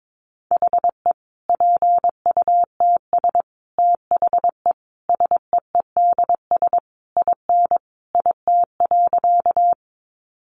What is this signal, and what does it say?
Morse code, heard as a keyed tone, about 22 wpm, 700 Hz. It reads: HE PUTS THE SEEDS IN IT.
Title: Easy Phrases 22wpm